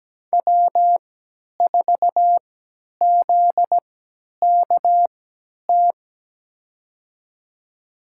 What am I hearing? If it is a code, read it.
Morse code, heard as a keyed tone, about 17 wpm, spaced as for 10 wpm, 700 Hz. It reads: W4ZKT
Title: USA Call Signs - 3x Character Spacing - 17wpm